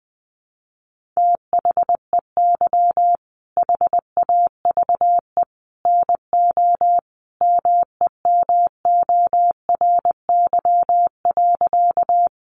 Morse code, heard as a keyed tone, about 20 wpm, 700 Hz. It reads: THEY HAVE NO MEMORY.
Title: Sentences from Top 1300 Words 20wpm